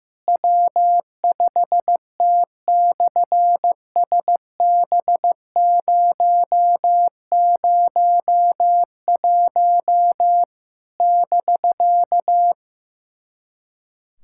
Morse code, heard as a keyed tone, about 15 wpm, 700 Hz. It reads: W5T/SB001 <BK>
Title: SOTA QSO Practice 15wpm